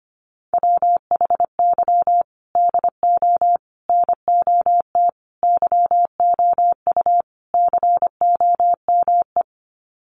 Morse code, heard as a keyed tone, about 25 wpm, 700 Hz. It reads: WHY DO NOT YOU COME
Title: Sets of 5 Words Encore2 25wpm